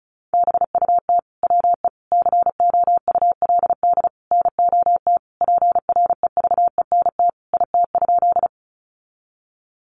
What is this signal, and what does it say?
Morse code, heard as a keyed tone, about 34 wpm, 700 Hz. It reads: BUT WE COULD NOT PREVENT IT?
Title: Sentences from Top 1100 Words 35wpm